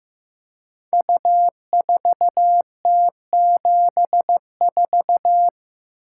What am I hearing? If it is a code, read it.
Morse code, heard as a keyed tone, about 15 wpm, 700 Hz. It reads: U4T74
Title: Code Groups - 5 Letter:Numbers - 15wpm